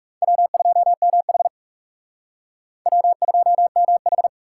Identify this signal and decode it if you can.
Morse code, heard as a keyed tone, about 40 wpm, 700 Hz. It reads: W2MH W2MH